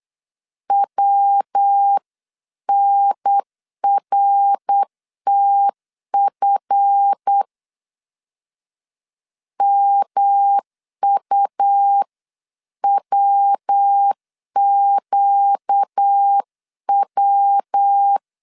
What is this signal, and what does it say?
Morse code, heard as a keyed tone, about 8 wpm, 800 Hz. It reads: WNRTF MUWQW